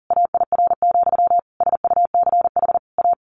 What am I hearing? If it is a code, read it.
Morse code, heard as a keyed tone, about 40 wpm, 700 Hz. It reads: AIR, SUCH A
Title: A Sound Of Thunder 40wpm